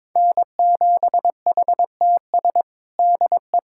Morse code, heard as a keyed tone, about 22 wpm, 700 Hz. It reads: N7HTS DE
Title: USA Callsign DE Callsign 22wpm